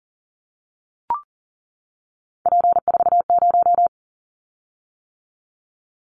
Morse code, heard as a keyed tone, about 37 wpm, 700 Hz. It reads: P40
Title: Callsign Prefixes for Top 100 DXCC - 40wpm